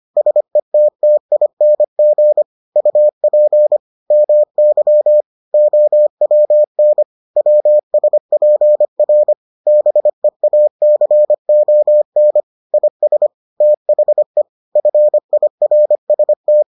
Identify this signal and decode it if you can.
Morse code, heard as a keyed tone, about 25 wpm, 600 Hz. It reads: SETTING UP MY OWN WSPR BEACON IS THE FIRST